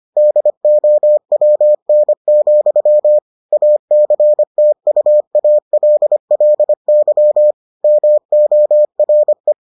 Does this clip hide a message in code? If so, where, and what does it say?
Yes: Morse code from 0.2 to 9.5 s, heard as a keyed tone, about 25 wpm, 600 Hz. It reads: DOWN, ACTUALLY MORE